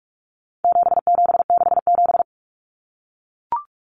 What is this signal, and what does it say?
Morse code, heard as a keyed tone, about 44 wpm, 700 Hz. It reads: 7767
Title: Numbers - 4-Digit 45wpm